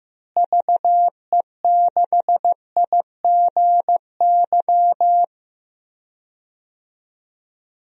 Morse code, heard as a keyed tone, about 15 wpm, 700 Hz. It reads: VE6IGY
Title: DXCC-100 Call Signs - 6 Characters - Repeated 2x - 15wpm